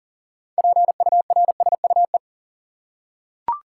Morse code, heard as a keyed tone, about 40 wpm, 700 Hz. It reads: PURSUE
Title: Six Letter Words - 40wpm